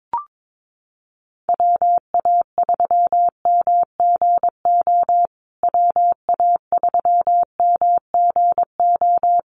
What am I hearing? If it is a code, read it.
Morse code, heard as a keyed tone, about 22 wpm, 700 Hz. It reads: WA3MGO WA3MGO
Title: USA Call Signs Repeated 3x 22wpm